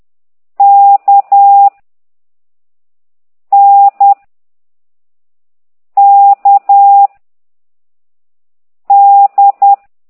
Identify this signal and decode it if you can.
Morse code, heard as a keyed tone, about 10 wpm, 800 Hz. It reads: K N K D
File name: ct2idw / exerc9